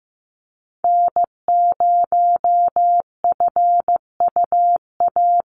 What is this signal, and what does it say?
Morse code, heard as a keyed tone, about 15 wpm, 700 Hz. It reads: N0FUA
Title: USA Call Signs 15wpm